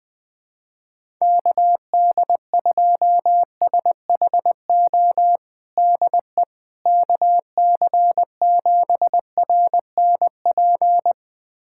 Morse code, heard as a keyed tone, about 21 wpm, 700 Hz. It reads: KD2SHO DE KC7RNP
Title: USA Callsign DE Callsign - 6 Characters - 20wpm